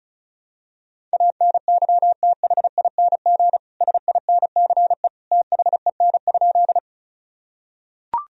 Morse code, heard as a keyed tone, about 34 wpm, 700 Hz. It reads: ANYTHING SINCE THEN?
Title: Sentences from Top 1000 Words - Review Part 1 - 35wpm